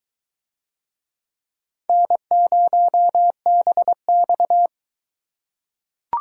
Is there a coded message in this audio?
Yes: Morse code from 1.9 to 4.7 s, heard as a keyed tone, about 22 wpm, 700 Hz. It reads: N0BX